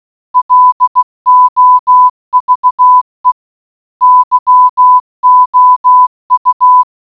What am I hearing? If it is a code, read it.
Morse code, heard as a keyed tone, about 16 wpm, 1000 Hz. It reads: LOVE YOU